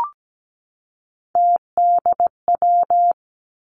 Morse code, heard as a keyed tone, about 17 wpm, 700 Hz. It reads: TDW
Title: Three Challenging Letter-Number Combinations 17wpm